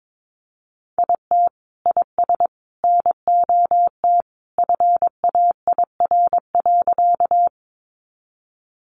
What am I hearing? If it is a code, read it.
Morse code, heard as a keyed tone, about 22 wpm, 700 Hz. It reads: IT IS NOT FAIR.
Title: Sentences from Top 1300 Words 22wpm